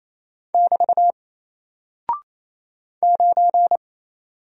Morse code, heard as a keyed tone, about 28 wpm, 700 Hz. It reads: <BT> 9
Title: Single Character (Letter:Number:Punctuation:Prosign) Alt 28wpm